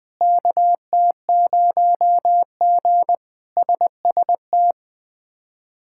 Morse code, heard as a keyed tone, about 20 wpm, 700 Hz. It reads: KT0G SST
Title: K1USN SST QSO Practice - 20wpm